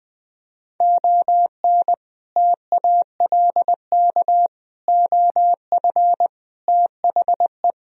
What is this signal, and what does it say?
Morse code, heard as a keyed tone, about 20 wpm, 700 Hz. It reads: ON TALK OF THE
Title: Speed-Racing - Sets of 5 Words 17wpm